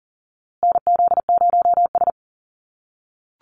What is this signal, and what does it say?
Morse code, heard as a keyed tone, about 40 wpm, 700 Hz. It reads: NZ0S
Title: USA Call Signs Repeated 3x 40wpm